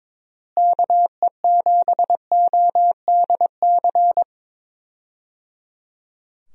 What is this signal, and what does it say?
Morse code, heard as a keyed tone, about 22 wpm, 700 Hz. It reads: KE7ODC
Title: USA Call Signs - 6 Characters - Repeated 3x - 22wpm